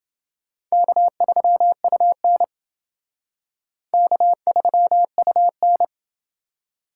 Morse code, heard as a keyed tone, about 29 wpm, 700 Hz. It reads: K3UN K3UN